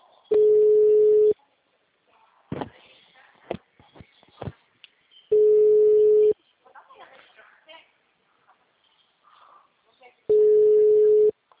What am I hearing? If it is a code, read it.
Ringback tone (Europe, 425 Hz), with speech.